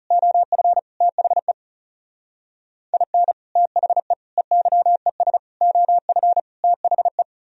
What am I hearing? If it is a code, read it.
Morse code, heard as a keyed tone, about 38 wpm, 700 Hz. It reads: OF THE IN THE EYES OF THE
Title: Speed-Racing - Sets of 5 Words 30wpm